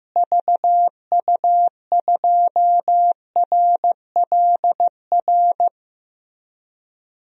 Morse code, heard as a keyed tone, about 14 wpm, 700 Hz. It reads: VU2RLR
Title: DXCC-100 Call Signs - 6 Characters - Repeated 3x - 15wpm